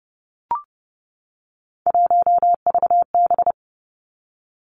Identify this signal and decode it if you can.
Morse code, heard as a keyed tone, about 30 wpm, 700 Hz. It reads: 1VB
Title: Three Challenging Letter-Number Combinations 30wpm